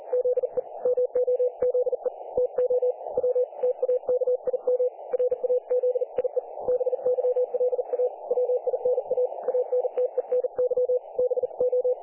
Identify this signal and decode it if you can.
Morse code, heard as a keyed tone, about 40 wpm, 500 Hz. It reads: ZE MOZE TO W TAKIM RAZIE BYLA WINA ANTENY BO
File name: dlazaawansowanych / nagraniazpasma / websdr recording 3564.7kHz